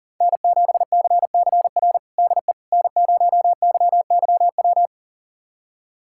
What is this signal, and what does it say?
Morse code, heard as a keyed tone, about 40 wpm, 700 Hz. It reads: N7CCR DE N0YYW
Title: USA Callsign DE Callsign - 5 Characters - 40wpm